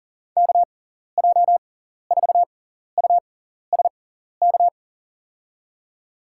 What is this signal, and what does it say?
Morse code, heard as a keyed tone, about 40 wpm, spaced as for 14 wpm, 700 Hz. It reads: KJ4USK
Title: USA Call Signs - 6 Characters - 6x Character Spacing - 40wpm